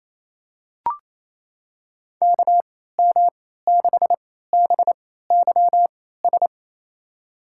Morse code, heard as a keyed tone, about 28 wpm, spaced as for 16 wpm, 700 Hz. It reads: KM6BYS
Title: USA Call Signs - 6 Characters - 3x Character Spacing - 28wpm